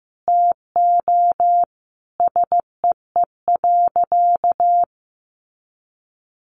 Morse code, heard as a keyed tone, about 15 wpm, 700 Hz. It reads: TO SEE.
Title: Sentences from Top 1200 Words 15wpm